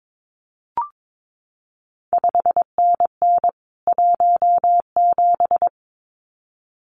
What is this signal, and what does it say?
Morse code, heard as a keyed tone, about 22 wpm, 700 Hz. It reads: 5NN 17
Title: CQ WW DX Contest Exchange Encore - 22wpm